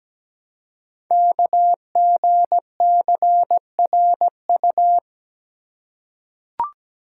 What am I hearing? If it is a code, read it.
Morse code, heard as a keyed tone, about 17 wpm, 700 Hz. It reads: KGCRU
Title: Code Groups - 5 Letters - 17wpm